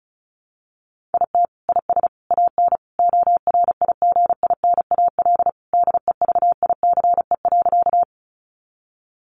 Morse code, heard as a keyed tone, about 34 wpm, 700 Hz. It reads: IT IS AN ORIGINAL DEVICE.
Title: Sentences from Top 1500 Words 35wpm